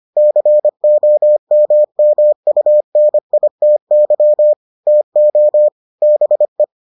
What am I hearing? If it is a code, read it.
Morse code, heard as a keyed tone, about 25 wpm, 600 Hz. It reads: COMMUNITY TO BE